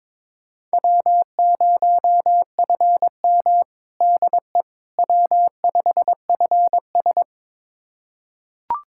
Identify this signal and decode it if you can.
Morse code, heard as a keyed tone, about 23 wpm, 700 Hz. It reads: W0FM DE W5FS
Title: USA Callsign DE Callsign - 4 Characters - 22wpm